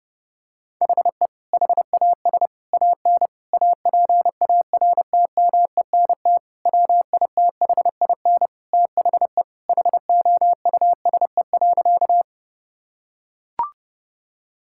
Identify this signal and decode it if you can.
Morse code, heard as a keyed tone, about 30 wpm, 700 Hz. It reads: HE HAS AN APARTMENT WITHIN THE HOUSE.
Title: Sentences from Top 1000 Words - Review Part 6 - 30wpm